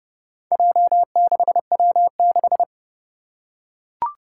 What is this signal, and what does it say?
Morse code, heard as a keyed tone, about 29 wpm, 700 Hz. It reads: J6W6
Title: Four Challenging Letter-Number Combinations 30wpm